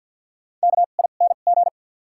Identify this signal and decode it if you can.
Morse code, heard as a keyed tone, about 48 wpm, spaced as for 35 wpm, 700 Hz. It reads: KING
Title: 2x Character Spacing - Top 500 Words - 50wpm